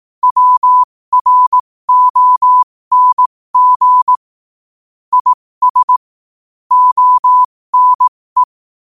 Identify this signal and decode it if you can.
Morse code, heard as a keyed tone, about 18 wpm, spaced as for 15 wpm, 1000 Hz. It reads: WRONG IS ONE
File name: qotdfiles / 15wpm